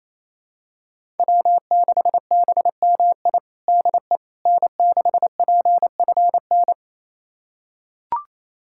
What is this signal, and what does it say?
Morse code, heard as a keyed tone, about 28 wpm, 700 Hz. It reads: W6BMI DE N6PFN
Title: USA Callsign DE Callsign - 5 Characters - 28wpm